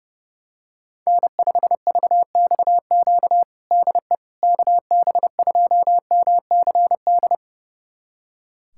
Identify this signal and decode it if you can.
Morse code, heard as a keyed tone, about 30 wpm, 700 Hz. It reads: N5VXQ DE KB2MCD
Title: USA Callsign DE Callsign 30wpm